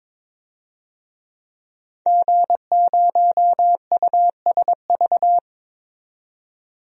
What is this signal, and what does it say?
Morse code, heard as a keyed tone, about 22 wpm, 700 Hz. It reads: G0USV
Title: DXCC-100 Call Signs - 5 Characters - 22wpm